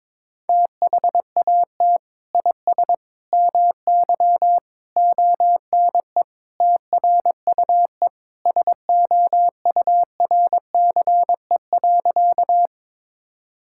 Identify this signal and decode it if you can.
Morse code, heard as a keyed tone, about 23 wpm, 700 Hz. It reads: THAT IS MY ONE TRUE SOURCE.
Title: Sentences from Top 1000 Words - Review Part 5 - 22wpm